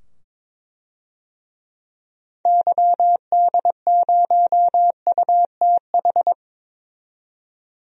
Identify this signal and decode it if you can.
Morse code, heard as a keyed tone, about 22 wpm, 700 Hz. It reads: YD0UTH
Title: DXCC-100 Call Signs - 6 Characters - 22wpm